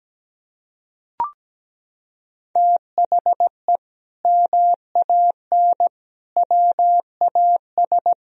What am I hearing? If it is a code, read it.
Morse code, heard as a keyed tone, about 17 wpm, 700 Hz. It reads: THE MAN WAS
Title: Sentences from Top 1000 Words - Review Part 4 - 17wpm